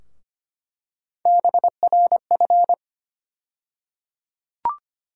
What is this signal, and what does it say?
Morse code, heard as a keyed tone, about 25 wpm, 700 Hz. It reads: BRF
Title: USA Call Sign Postfixes - 3 Letters 25wpm